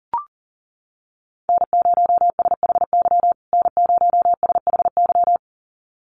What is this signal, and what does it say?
Morse code, heard as a keyed tone, about 40 wpm, 700 Hz. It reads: N0SHY N0SHY
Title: USA Call Signs Repeated 2x 40wpm